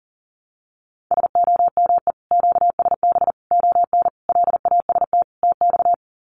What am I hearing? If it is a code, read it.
Morse code, heard as a keyed tone, about 40 wpm, 700 Hz. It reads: SOME QSB ON LAST TX
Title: Common QSO Phrases 40wpm